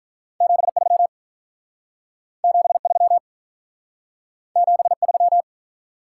Morse code, heard as a keyed tone, about 45 wpm, 700 Hz. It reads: 73 73 73